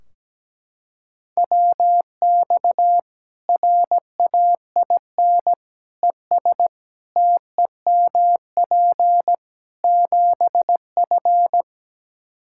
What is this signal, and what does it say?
Morse code, heard as a keyed tone, about 17 wpm, 700 Hz. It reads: WX RAIN ES TEMP 7F
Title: Practice QSO - Exchange 2 - Info - 17wpm